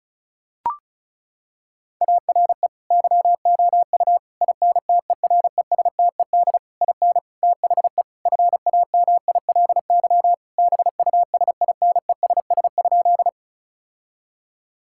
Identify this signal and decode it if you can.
Morse code, heard as a keyed tone, about 34 wpm, 700 Hz. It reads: ARE YOU INTERESTED IN THE FAMILY BUSINESS?
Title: Sentences from Top 1000 Words - Review Part 6 - 35wpm